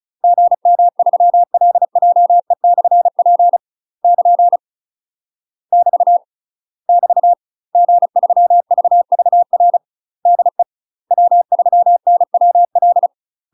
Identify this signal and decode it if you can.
Morse code, heard as a keyed tone, about 35 wpm, 700 Hz. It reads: GM3LJE/P <KN> <BT> <BT> G3VVR DE W3NWL
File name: UK MASTER QSOs TRAINING / 35WPM / UK MASTER QSOs 0015